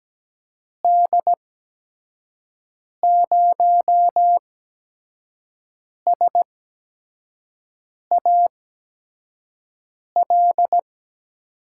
Morse code, heard as a keyed tone, about 17 wpm, spaced as for 5 wpm, 700 Hz. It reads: D0SAL
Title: USA Call Signs - 6 Characters - 8x Character Spacing - 17wpm